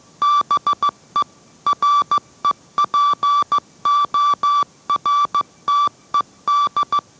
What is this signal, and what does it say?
Morse code, heard as a keyed tone, about 15 wpm, 1160 Hz. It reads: BE REPORTED